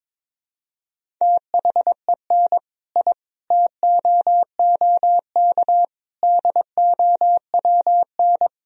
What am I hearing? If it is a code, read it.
Morse code, heard as a keyed tone, about 22 wpm, 700 Hz. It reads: THEN I TOOK DOWN
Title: Sentences from Top 1000 Words - Review Part 1 - 22wpm